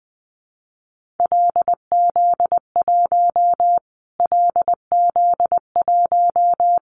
Morse code, heard as a keyed tone, about 20 wpm, 700 Hz. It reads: LZ1 LZ1
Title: Callsign Prefixes for Top 25 DXCC - Repeated 3x - 20wpm